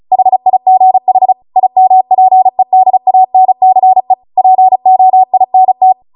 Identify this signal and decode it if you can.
Morse code, heard as a keyed tone, about 36 wpm, 750 Hz. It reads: HIGH IMPEDANCE POINT